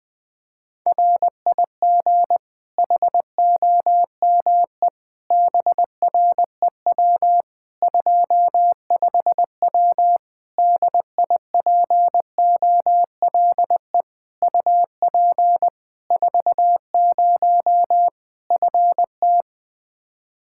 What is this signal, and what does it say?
Morse code, heard as a keyed tone, about 20 wpm, 700 Hz. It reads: RIG HOME BREW 25W DIPOLE UP 40 FT
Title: K1USN SST QSO Practice 20wpm